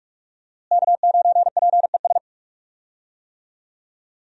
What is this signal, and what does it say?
Morse code, heard as a keyed tone, about 46 wpm, 700 Hz. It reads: K9PES
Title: USA Call Signs - 5 Characters - Repeated 2x - 45wpm